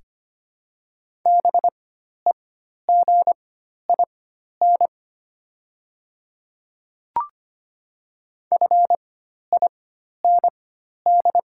Morse code, heard as a keyed tone, about 25 wpm, spaced as for 12 wpm, 690 Hz. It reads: BEGIN FIND
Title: 4x Character Spacing - Top 300 Words - 25wpm